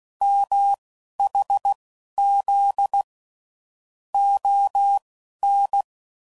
Morse code, heard as a keyed tone, about 16 wpm, spaced as for 11 wpm, 800 Hz. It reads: MHZ ON
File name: CW Code Groups @10WPM / QSOs / QSO kc7sac